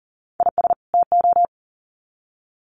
Morse code, heard as a keyed tone, about 40 wpm, 700 Hz. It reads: IS TO
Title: Sets of 2 Words 40wpm